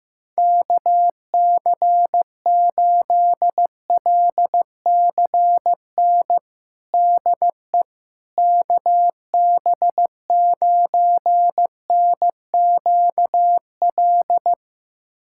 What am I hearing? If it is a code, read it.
Morse code, heard as a keyed tone, about 15 wpm, 700 Hz. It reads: KC8LCN DE KB9NQL
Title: USA Callsign DE Callsign - 6 Characters - 15wpm